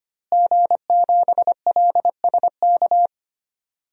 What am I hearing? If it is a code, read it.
Morse code, heard as a keyed tone, about 25 wpm, 700 Hz. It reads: G7LSK
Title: DXCC-100 Call Signs - 5 Characters - Repeated 2x - 25wpm